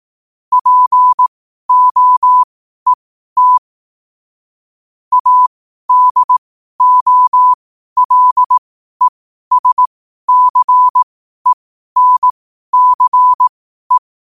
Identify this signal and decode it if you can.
Morse code, heard as a keyed tone, about 18 wpm, spaced as for 12 wpm, 1000 Hz. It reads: POET ADOLESCENCE
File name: qotdfiles / 12wpm